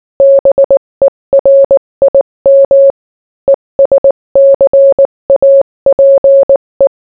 Morse code, heard as a keyed tone, about 19 wpm, 550 Hz. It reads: BERIM ESCAPE